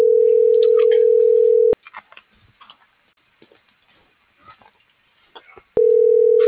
Ringback tone (North America), 440 + 480 Hz.